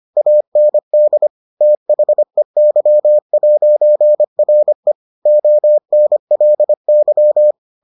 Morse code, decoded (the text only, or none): AND THEY'RE ONLY